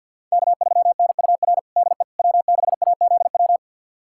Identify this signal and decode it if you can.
Morse code, heard as a keyed tone, about 51 wpm, 700 Hz. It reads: K3NUR DE W6AZW
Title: USA Callsign DE Callsign - 5 Characters - 50wpm